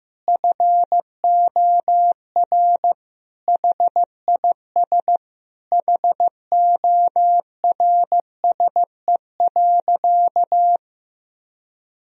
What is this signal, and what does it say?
Morse code, heard as a keyed tone, about 15 wpm, 700 Hz. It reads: FOR HIS HORSE.
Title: Sentences from Top 1000 Words - Review Part 3 - 15wpm